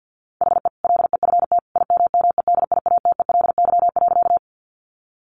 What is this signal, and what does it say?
Morse code, heard as a keyed tone, about 50 wpm, 700 Hz. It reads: HE LEFT IMMEDIATELY.
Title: Sentences from Top 1100 Words 50wpm